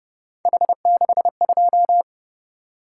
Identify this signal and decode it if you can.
Morse code, heard as a keyed tone, about 30 wpm, 700 Hz. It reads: H62
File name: Speed-Racing - Three Letter-Number Combinations - 25wpm